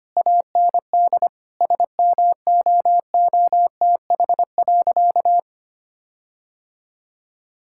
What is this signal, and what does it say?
Morse code, heard as a keyed tone, about 25 wpm, 700 Hz. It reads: AND SMOOTH.
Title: Sentences from Top 1600 Words 25wpm